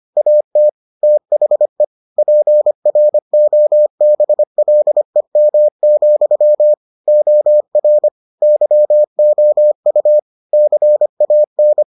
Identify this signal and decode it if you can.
Morse code, heard as a keyed tone, about 25 wpm, 600 Hz. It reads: AT THE PROBLEM, OR YOU CAN